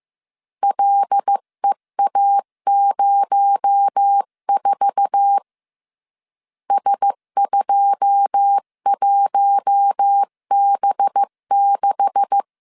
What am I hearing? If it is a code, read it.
Morse code, heard as a keyed tone, about 15 wpm, 800 Hz. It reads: LEA04 S21B6
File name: CW / DET / 1 9 1500C S B 14wpm 3302